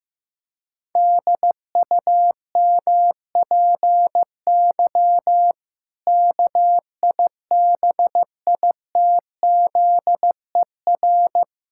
Morse code, heard as a keyed tone, about 15 wpm, 700 Hz. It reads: DUMPY KIBITZER
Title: Pangram Challenge v2 - 15wpm